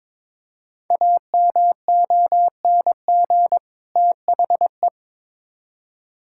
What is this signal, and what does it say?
Morse code, heard as a keyed tone, about 22 wpm, 700 Hz. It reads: AMONG THE
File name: Build A Phrase - 2-Word Phrases - 22wpm